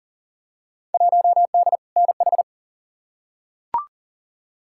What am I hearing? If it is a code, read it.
Morse code, heard as a keyed tone, about 40 wpm, 700 Hz. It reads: 1D NH
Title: Field Day Exchanges Encore - 40wpm